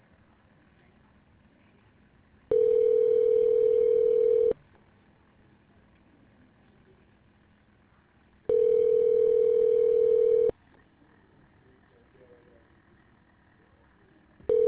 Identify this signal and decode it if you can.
Ringback tone (North America), 440 + 480 Hz.